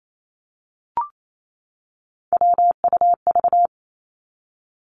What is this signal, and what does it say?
Morse code, heard as a keyed tone, about 28 wpm, 700 Hz. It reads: WUV
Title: Three Challenging Letter-Number Combinations 28wpm